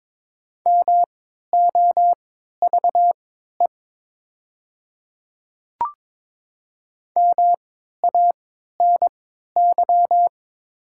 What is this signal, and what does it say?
Morse code, heard as a keyed tone, about 22 wpm, spaced as for 13 wpm, 700 Hz. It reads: MOVE MANY
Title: 3x Character Spacing - Top 200 Words - 22wpm